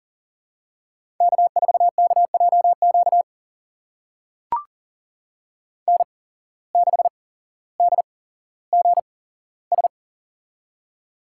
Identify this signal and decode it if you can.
Morse code, heard as a keyed tone, about 39 wpm, spaced as for 17 wpm, 700 Hz. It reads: K4KJQ N6DGS
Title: USA Call Signs - 5 Characters - 8x Character Spacing - 40wpm